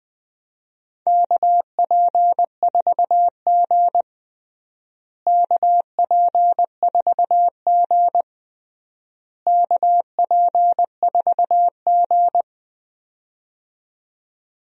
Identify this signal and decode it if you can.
Morse code, heard as a keyed tone, about 20 wpm, 700 Hz. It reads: KP4G KP4G KP4G
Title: DXCC-100 Call Signs - 4 Characters - Repeated 3x - 20wpm